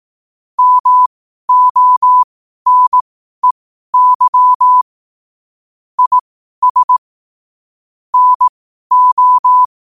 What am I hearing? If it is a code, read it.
Morse code, heard as a keyed tone, about 18 wpm, spaced as for 12 wpm, 1000 Hz. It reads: MONEY IS NO
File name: qotdfiles / 12wpm